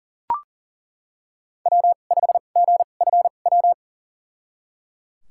Morse code, heard as a keyed tone, about 38 wpm, spaced as for 28 wpm, 700 Hz. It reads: W5GFW
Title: USA Call Signs - 5 Characters - 2x Character Spacing - 40wpm